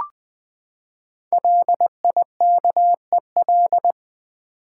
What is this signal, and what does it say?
Morse code, heard as a keyed tone, about 20 wpm, 700 Hz. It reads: LIKEL
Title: Build A Word - 5 Letters At A Time - 10-Letter Words - 20wpm